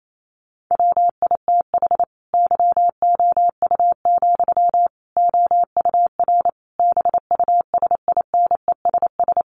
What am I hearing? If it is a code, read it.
Morse code, heard as a keyed tone, about 28 wpm, 700 Hz. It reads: WITH YOU, OUR BUSINESS
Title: Sentences from Top 1400 Words 28wpm